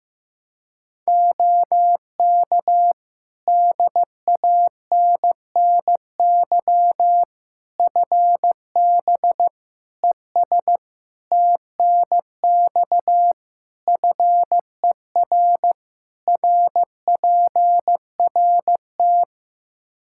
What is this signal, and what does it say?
Morse code, heard as a keyed tone, about 16 wpm, 700 Hz. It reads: OK DANNY FB ES TNX FER RPRT
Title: Practice QSO - Exchange 2 - Info - 15wpm